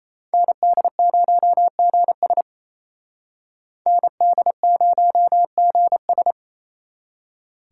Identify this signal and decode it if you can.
Morse code, heard as a keyed tone, about 35 wpm, 700 Hz. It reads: ND0GS ND0GS
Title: Speed Racing - USA Call Signs 28wpm